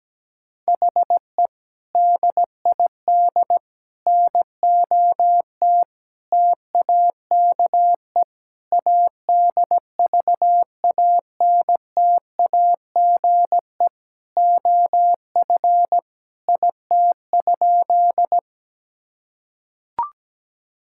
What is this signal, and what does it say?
Morse code, heard as a keyed tone, about 17 wpm, 700 Hz. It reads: HE DID NOT TAKE ADVANTAGE OF IT?
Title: Sentences from Top 1000 Words - Review Part 6 - 17wpm